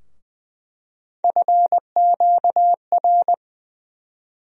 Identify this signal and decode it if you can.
Morse code, heard as a keyed tone, about 21 wpm, 700 Hz. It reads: FQR
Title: USA Call Sign Postfixes - 3 Letters 20wpm